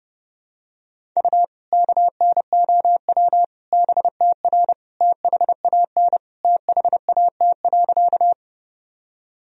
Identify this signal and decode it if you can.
Morse code, heard as a keyed tone, about 30 wpm, 700 Hz. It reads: U KNOW BTR THAN THAT.
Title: Sentences from Top 400 Words and Abbreviations 30wpm